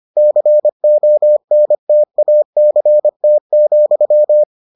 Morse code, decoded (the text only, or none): CONTACT,